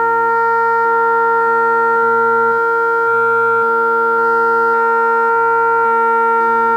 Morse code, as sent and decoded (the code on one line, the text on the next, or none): none